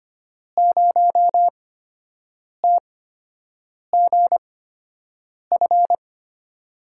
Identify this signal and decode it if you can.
Morse code, heard as a keyed tone, about 25 wpm, spaced as for 7 wpm, 700 Hz. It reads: 0TGF
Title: USA Call Signs - 5 Characters - 8x Character Spacing - 25wpm